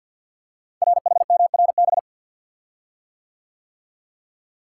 Morse code, heard as a keyed tone, about 50 wpm, 700 Hz. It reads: RHMRB
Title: Code Groups - 5 Letters - 50wpm